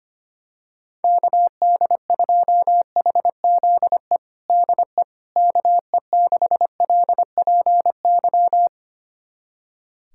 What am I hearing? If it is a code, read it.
Morse code, heard as a keyed tone, about 25 wpm, 700 Hz. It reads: KD2HZE DE KE6LPY
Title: USA Callsign DE Callsign - 6 Characters - 25wpm